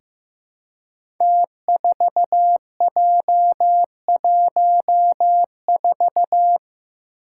Morse code, heard as a keyed tone, about 15 wpm, 700 Hz. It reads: T4J14
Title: Five Challenging Letter-Number Combinations 15wpm